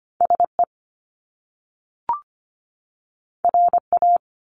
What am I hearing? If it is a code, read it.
Morse code, heard as a keyed tone, about 25 wpm, 700 Hz. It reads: SE RA
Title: Top Two Letter Combinations in Words 25wpm